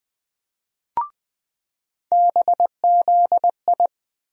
Morse code, heard as a keyed tone, about 20 wpm, 700 Hz. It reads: BZI
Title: USA Call Sign Postfixes - 3 Letters 20wpm